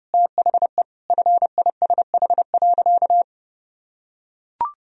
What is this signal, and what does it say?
Morse code, heard as a keyed tone, about 30 wpm, 700 Hz. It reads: THE FISH.
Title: Sentences from Top 1000 Words - Review Part 1 - 30wpm